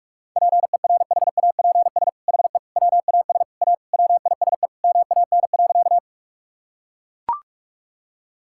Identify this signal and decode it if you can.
Morse code, heard as a keyed tone, about 41 wpm, 700 Hz. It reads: PERHAPS HE WAS A WISE MAN.
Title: Sentences from Top 1600 Words 45wpm